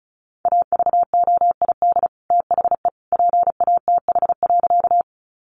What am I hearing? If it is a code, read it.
Morse code, heard as a keyed tone, about 34 wpm, 700 Hz. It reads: AVOID THE PATH.
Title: Sentences from Top 1400 Words 35wpm